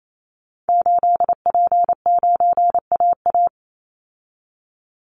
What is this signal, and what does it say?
Morse code, heard as a keyed tone, about 28 wpm, 700 Hz. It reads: 8P9AA
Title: Call Signs - 28wpm